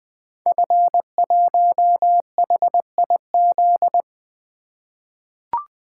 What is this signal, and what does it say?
Morse code, heard as a keyed tone, about 20 wpm, 700 Hz. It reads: F1HIZ
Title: Code Groups - 5 Letter:Numbers - 20wpm